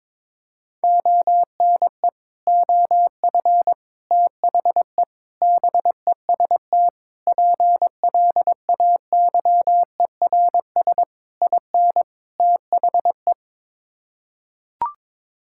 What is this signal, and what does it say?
Morse code, heard as a keyed tone, about 22 wpm, 700 Hz. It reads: ONE OF THE BEST PLAYERS IN THE
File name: practice / 22wpm / Sets of 7 Words - 22wpm